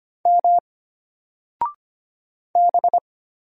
Morse code, heard as a keyed tone, about 25 wpm, 700 Hz. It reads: M B
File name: Single-Letter-Number-25wpm